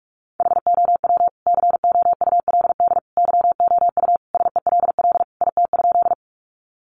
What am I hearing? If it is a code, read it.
Morse code, heard as a keyed tone, about 44 wpm, 700 Hz. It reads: HOW COULD YOU SELL IT?
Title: Sentences from Top 1100 Words 45wpm